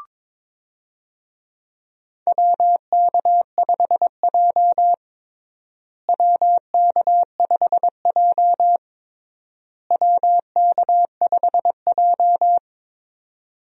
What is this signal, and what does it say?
Morse code, heard as a keyed tone, about 22 wpm, 700 Hz. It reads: WK5J WK5J WK5J
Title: USA Call Signs - 4 Characters - Repeated 3x - 22wpm